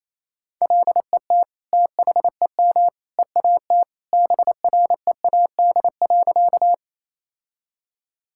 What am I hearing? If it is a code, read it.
Morse code, heard as a keyed tone, about 28 wpm, 700 Hz. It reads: LET THEM EAT BREAD.
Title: Sentences from Top 1600 Words 28wpm